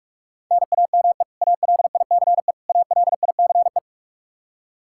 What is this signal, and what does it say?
Morse code, heard as a keyed tone, about 46 wpm, 700 Hz. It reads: NAME ALICE ALICE
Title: Practice QSO - Exchange 1 - Beginning - 45wpm